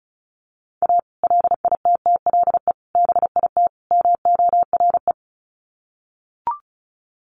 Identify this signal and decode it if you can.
Morse code, heard as a keyed tone, about 34 wpm, 700 Hz. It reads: A LITTLE BIT MORE
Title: Sets of 4 Words 35wpm